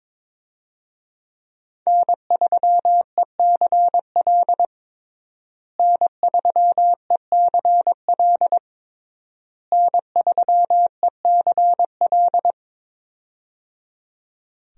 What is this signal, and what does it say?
Morse code, heard as a keyed tone, about 22 wpm, 700 Hz. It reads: N3ECL N3ECL N3ECL
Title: USA Call Signs - 5 Characters - Repeated 3x - 22wpm